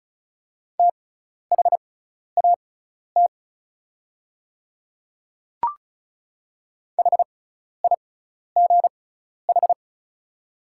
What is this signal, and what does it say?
Morse code, heard as a keyed tone, about 36 wpm, spaced as for 12 wpm, 690 Hz. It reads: THAT HIGH
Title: 6x Character Spacing - Top 300 Words - 35wpm